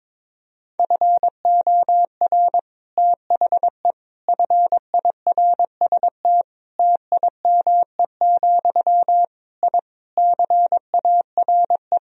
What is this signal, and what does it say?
Morse code, heard as a keyed tone, about 22 wpm, 700 Hz. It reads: FOR THE FIRST TIME, I CARE
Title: Sentences from Top 1000 Words - Review Part 2 - 22wpm